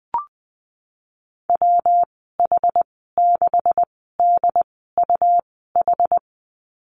Morse code, heard as a keyed tone, about 19 wpm, spaced as for 14 wpm, 700 Hz. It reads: WH6DUH
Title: 147 Lesson 24g - Letter y - Call Signs 20wpm